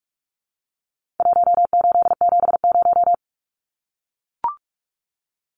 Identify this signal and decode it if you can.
Morse code, heard as a keyed tone, about 44 wpm, 700 Hz. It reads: 1870
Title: Numbers - Contest Style - 4-Digit 45wpm